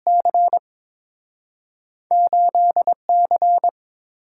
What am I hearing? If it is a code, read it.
Morse code, heard as a keyed tone, about 30 wpm, 700 Hz. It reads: C 8C